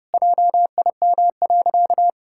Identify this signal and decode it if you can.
Morse code, heard as a keyed tone, about 30 wpm, 700 Hz. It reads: JIM.
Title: Pangram Challenge v2 - 30wpm